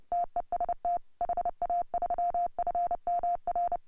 Morse code, heard as a keyed tone, about 30 wpm, 690 Hz. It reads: TEST HA3FMR
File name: AUDIO2018 / HA3FMR CW